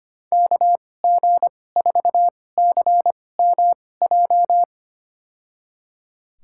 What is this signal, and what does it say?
Morse code, heard as a keyed tone, about 25 wpm, spaced as for 18 wpm, 700 Hz. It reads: KG4CMJ
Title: USA Call Signs - 2x Character Spacing - 25wpm